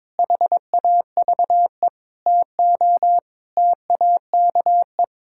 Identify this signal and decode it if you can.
Morse code, heard as a keyed tone, about 22 wpm, 700 Hz. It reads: HAVE TO TAKE
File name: Sentences with 7-Word Phrases - 22wpm